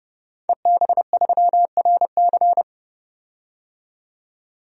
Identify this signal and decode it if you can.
Morse code, heard as a keyed tone, about 30 wpm, 700 Hz. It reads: EB3RC
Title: DXCC-100 Call Signs - 5 Characters - 30wpm